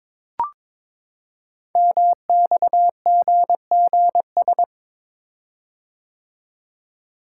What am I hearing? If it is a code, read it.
Morse code, heard as a keyed tone, about 23 wpm, 700 Hz. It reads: MXGGS
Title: Code Groups - 5 Letters - 22wpm